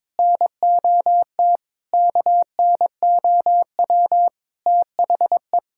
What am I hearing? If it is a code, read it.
Morse code, heard as a keyed tone, about 22 wpm, 700 Hz. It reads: NOT KNOW THE
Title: Sentences from Top 1000 Words - Review Part 6 - 22wpm